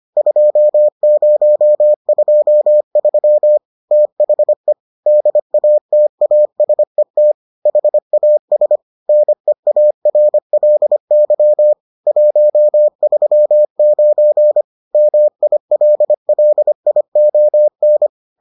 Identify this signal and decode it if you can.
Morse code, heard as a keyed tone, about 25 wpm, 600 Hz. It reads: 2023 THE DATASET HAS NEARLY 139 MILLION